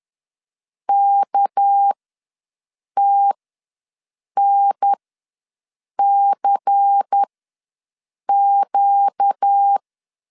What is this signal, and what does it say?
Morse code, heard as a keyed tone, about 10 wpm, spaced as for 6 wpm, 800 Hz. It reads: KTNCQ
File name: LEZIONI / 6 PKX0 / 6 PK ws 10wpm 3745